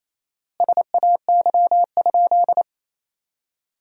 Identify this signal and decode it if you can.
Morse code, heard as a keyed tone, about 28 wpm, 700 Hz. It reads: SAY?
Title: 4x Word-Spacing - Sentences from Top 500 Words - 28wpm